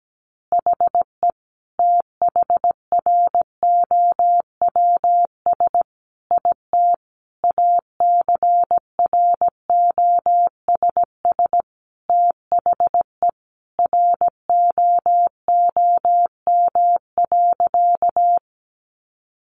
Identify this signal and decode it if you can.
Morse code, heard as a keyed tone, about 17 wpm, 700 Hz. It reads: HE THROWS IT ACROSS THE ROOM.
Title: Additional Easy Phrases 17wpm